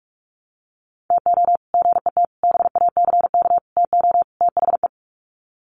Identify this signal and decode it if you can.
Morse code, heard as a keyed tone, about 44 wpm, 700 Hz. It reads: TO GET BACK TO THE
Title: Sets of 5 Words Encore 45wpm